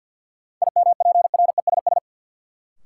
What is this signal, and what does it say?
Morse code, heard as a keyed tone, about 47 wpm, 700 Hz. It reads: IMPRESS
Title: Sentences from Top 1600 Words - New Word Drill - 50wpm